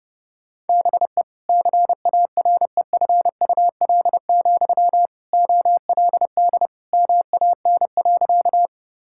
Morse code, heard as a keyed tone, about 30 wpm, 700 Hz. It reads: BE CAREFUL, OLD MAN.
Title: Sentences from Top 1600 Words 30wpm